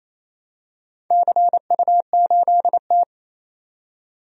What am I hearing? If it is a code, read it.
Morse code, heard as a keyed tone, about 28 wpm, 700 Hz. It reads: CU8T
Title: DXCC-100 Call Signs - 4 Characters - 28wpm